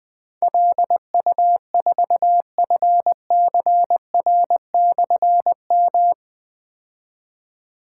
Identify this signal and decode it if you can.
Morse code, heard as a keyed tone, about 20 wpm, 700 Hz. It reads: LU4FCR/M
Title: Difficult Call Signs 20wpm